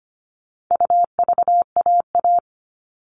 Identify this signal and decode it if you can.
Morse code, heard as a keyed tone, about 25 wpm, 700 Hz. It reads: UVAA
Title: All Cut Numbers - 4-Digit 25wpm